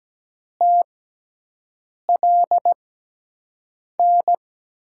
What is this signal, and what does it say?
Morse code, heard as a keyed tone, about 17 wpm, spaced as for 6 wpm, 700 Hz. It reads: TLN
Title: USA Call Signs - 5 Characters - 6x Character Spacing - 17wpm